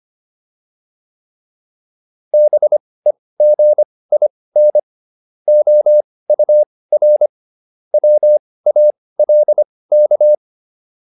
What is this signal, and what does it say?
Morse code, heard as a keyed tone, about 25 wpm, spaced as for 18 wpm, 600 Hz. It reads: BEGIN OUR WALK
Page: intermediate-practice-files